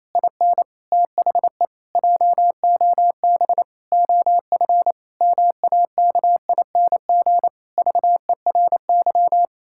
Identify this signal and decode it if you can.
Morse code, heard as a keyed tone, about 28 wpm, 700 Hz. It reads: IN THE JOB OF MAKING VERY
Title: Pangram Challenge v2 - 28wpm